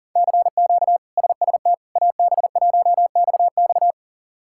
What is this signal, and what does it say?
Morse code, heard as a keyed tone, about 40 wpm, 700 Hz. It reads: CQ SST AB1XX
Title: K1USN SST QSO Practice - 40wpm